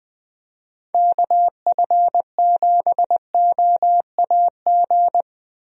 Morse code, heard as a keyed tone, about 21 wpm, 700 Hz. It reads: KF7OAG
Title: USA Call Signs - 6 Characters - 20wpm